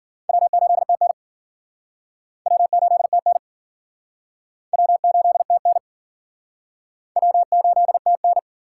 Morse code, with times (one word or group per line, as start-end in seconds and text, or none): 0.3-1.1 W8TN
2.5-3.4 W8TN
4.7-5.8 W8TN
7.2-8.4 W8TN